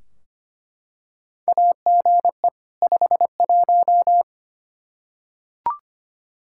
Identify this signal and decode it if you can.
Morse code, heard as a keyed tone, about 25 wpm, 700 Hz. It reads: AGE 51
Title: Practice QSO - Exchange 3 - More Info - 25wpm